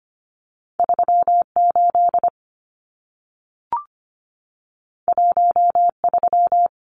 Morse code, heard as a keyed tone, about 25 wpm, 700 Hz. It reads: 38 13
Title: Numbers - 2-Digit 25wpm